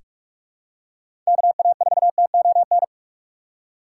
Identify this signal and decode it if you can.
Morse code, heard as a keyed tone, about 46 wpm, 700 Hz. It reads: KA4TON
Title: USA Call Signs - 6 Characters - 8x Character Spacing - 45wpm